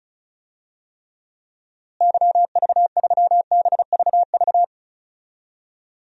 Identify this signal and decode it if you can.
Morse code, heard as a keyed tone, about 36 wpm, 700 Hz. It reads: YV3BVV
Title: DXCC-100 Call Signs - 6 Characters - Repeated 3x - 35wpm